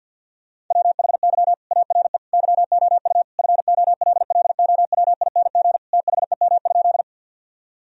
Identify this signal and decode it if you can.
Morse code, heard as a keyed tone, about 50 wpm, 700 Hz. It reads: WHY ARE YOU FOLLOWING THEM?
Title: Sentences from Top 1000 Words - Review Part 4 - 50wpm